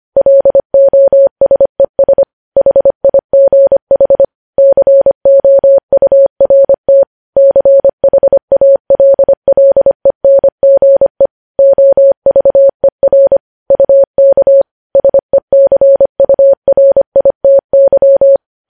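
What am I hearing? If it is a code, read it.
Morse code, heard as a keyed tone, about 25 wpm, 550 Hz. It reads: LOSES HIGH COURT CHALLENGE OVER UK SECURITY
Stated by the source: Morse code news headlines sent at 25 wpm.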